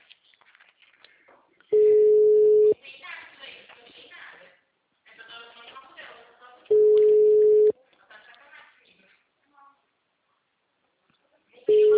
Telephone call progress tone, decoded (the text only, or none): ringback tone